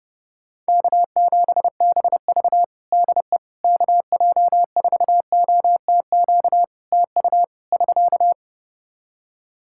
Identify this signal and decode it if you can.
Morse code, heard as a keyed tone, about 29 wpm, 700 Hz. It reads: K7BV DE KJ4OTQ TU <SK>
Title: Practice QSO - Exchange 4 - Ending - 30wpm